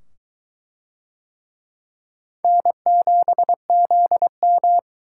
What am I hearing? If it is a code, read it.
Morse code, heard as a keyed tone, about 22 wpm, 700 Hz. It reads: N7ZM